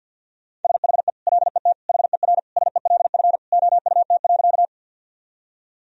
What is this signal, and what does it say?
Morse code, heard as a keyed tone, about 47 wpm, 700 Hz. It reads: SHE LET HERSELF OUT.
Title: Sentences from Top 1000 Words - Review Part 1 - 50wpm